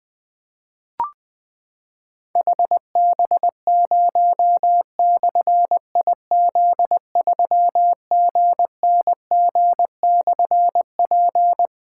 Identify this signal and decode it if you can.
Morse code, heard as a keyed tone, about 21 wpm, 700 Hz. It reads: HB0/IZ3GNG/P
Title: Difficult Call Signs 20wpm